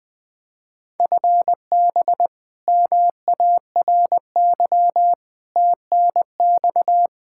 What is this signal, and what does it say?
Morse code, heard as a keyed tone, about 21 wpm, 700 Hz. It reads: FB MARY TNX
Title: Practice QSO - Exchange 3 - More Info - 20wpm